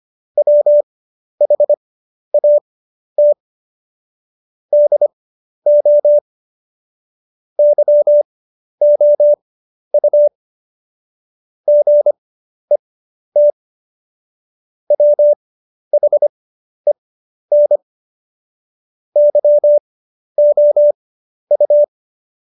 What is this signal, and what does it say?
Morse code, heard as a keyed tone, about 25 wpm, spaced as for 11 wpm, 590 Hz. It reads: WHAT DO YOU GET WHEN YOU
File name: Jokes-25-char-10-eff